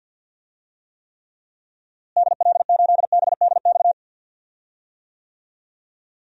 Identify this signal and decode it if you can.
Morse code, heard as a keyed tone, about 47 wpm, 700 Hz. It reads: DL8BDX
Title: DXCC-100 Call Signs - 6 Characters - Repeated 3x - 50wpm